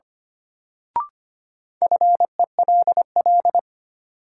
Morse code, heard as a keyed tone, about 25 wpm, 700 Hz. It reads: FELL
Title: Sentences from Top 1600 Words - New Word Drill - 25wpm